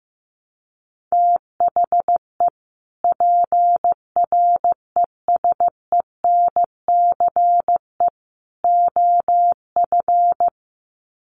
Morse code, heard as a keyed tone, about 15 wpm, 700 Hz. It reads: THE PRESENCE OF
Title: Sets of 3 Words 15wpm